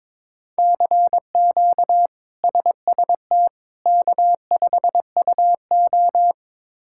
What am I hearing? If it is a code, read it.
Morse code, heard as a keyed tone, about 22 wpm, 700 Hz. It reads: CQ SST K5UO
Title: K1USN SST QSO Practice - 22wpm